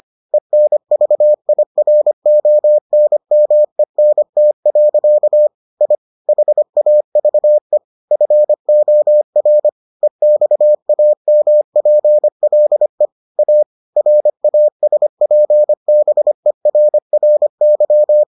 Morse code, heard as a keyed tone, about 25 wpm, 600 Hz. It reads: ENVIRONMENT. I HAVE FOR EXAMPLE A RASPBERRY